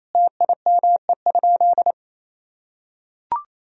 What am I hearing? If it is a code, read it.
Morse code, heard as a keyed tone, about 27 wpm, 700 Hz. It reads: TIME?
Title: Sentences from Top 1000 Words - Review Part 3 - 28wpm